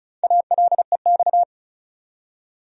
Morse code, heard as a keyed tone, about 36 wpm, 700 Hz. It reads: ALEX
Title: USA Common Names - Max 4 Letters - 35wpm